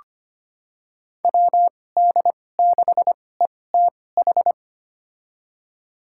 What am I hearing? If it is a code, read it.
Morse code, heard as a keyed tone, about 25 wpm, spaced as for 18 wpm, 700 Hz. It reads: WD6ETH
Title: USA Call Signs - 2x Character Spacing - 25wpm